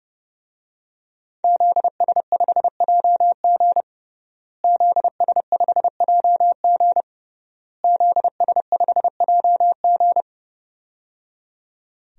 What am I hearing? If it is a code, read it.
Morse code, heard as a keyed tone, about 30 wpm, 700 Hz. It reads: ZS5JG ZS5JG ZS5JG
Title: DXCC-100 Call Signs - 5 Characters - Repeated 3x - 30wpm